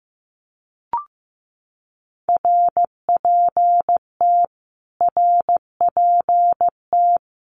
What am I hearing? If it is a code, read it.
Morse code, heard as a keyed tone, about 15 wpm, 700 Hz. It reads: RPT RPT
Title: Abbreviations Repeated 3x - 15wpm